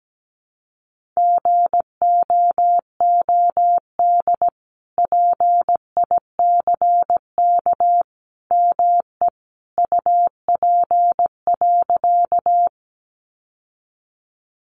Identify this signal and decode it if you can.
Morse code, heard as a keyed tone, about 17 wpm, 700 Hz. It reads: GOOD PICK ME UP.
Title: Sentences from Top 1000 Words 17wpm